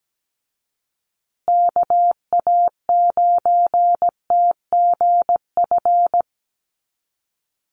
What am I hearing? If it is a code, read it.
Morse code, heard as a keyed tone, about 17 wpm, 700 Hz. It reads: KA9TGF
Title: USA Call Signs Repeated 2x 17wpm